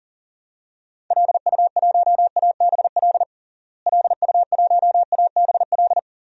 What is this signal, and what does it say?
Morse code, heard as a keyed tone, about 40 wpm, 700 Hz. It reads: LU1ABL LU1ABL
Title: DXCC-100 Call Signs - 6 Characters - Repeated 3x - 40wpm